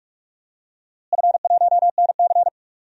Morse code, heard as a keyed tone, about 46 wpm, 700 Hz. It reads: F1NC
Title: DXCC-100 Call Signs - 4 Characters - Repeated 2x - 45wpm